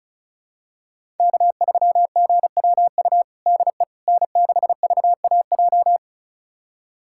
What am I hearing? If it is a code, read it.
Morse code, heard as a keyed tone, about 34 wpm, 700 Hz. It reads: K3GWU DE N6VAJ
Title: USA Callsign DE Callsign - 5 Characters - 35wpm